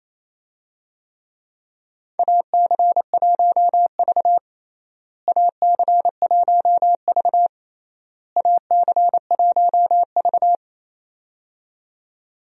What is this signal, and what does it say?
Morse code, heard as a keyed tone, about 28 wpm, 700 Hz. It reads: AC1V AC1V AC1V
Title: USA Call Signs - 4 Characters - Repeated 3x - 28wpm